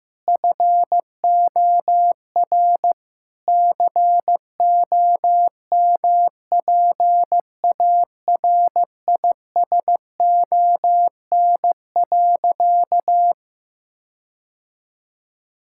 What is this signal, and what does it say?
Morse code, heard as a keyed tone, about 16 wpm, 700 Hz. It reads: FOR COMPARISON.
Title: Sentences from Top 1600 Words 15wpm